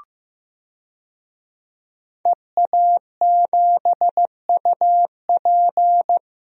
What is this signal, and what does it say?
Morse code, heard as a keyed tone, about 15 wpm, 700 Hz. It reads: EA7UP
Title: DXCC-100 Call Signs - 5 Characters - 15wpm